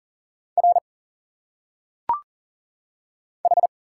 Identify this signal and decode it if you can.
Morse code, heard as a keyed tone, about 40 wpm, 700 Hz. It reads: R H
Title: Single Letters - Phonetic Pronunciation 40wpm